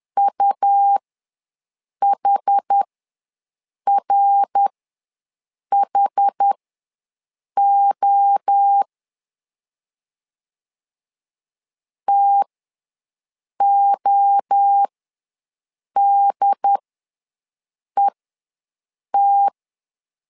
Morse code, heard as a keyed tone, about 10 wpm, spaced as for 6 wpm, 800 Hz. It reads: UHRHO TODET
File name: LEZIONI / 3 MWRO / 3 MWRO ws 10wpm 2310